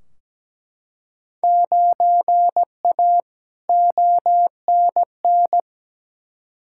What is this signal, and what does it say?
Morse code, heard as a keyed tone, about 17 wpm, 700 Hz. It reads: 9A ONN
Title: Field Day Exchanges Encore - 17wpm